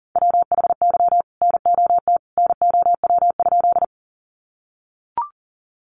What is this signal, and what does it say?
Morse code, heard as a keyed tone, about 40 wpm, 700 Hz. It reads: WHY NOT NOW?
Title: Sentences from Top 600 Words 40wpm